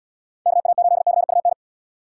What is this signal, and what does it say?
Morse code, heard as a keyed tone, about 70 wpm, 700 Hz. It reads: KA8CUN